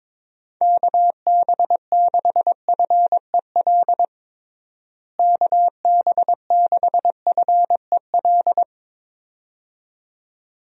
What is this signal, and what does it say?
Morse code, heard as a keyed tone, about 22 wpm, 700 Hz. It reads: KB6FEL KB6FEL
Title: USA Call Signs - 6 Characters - Repeated 2x - 22wpm